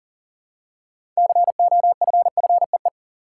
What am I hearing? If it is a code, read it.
Morse code, heard as a keyed tone, about 40 wpm, 700 Hz. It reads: COFFEE
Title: Six Letter Words - 40wpm